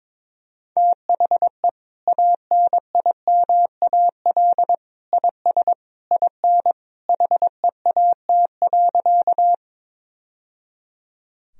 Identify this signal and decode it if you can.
Morse code, heard as a keyed tone, about 22 wpm, 700 Hz. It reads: THE ANIMAL IS IN HEAT.
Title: Sentences from Top 1000 Words - Review Part 3 - 22wpm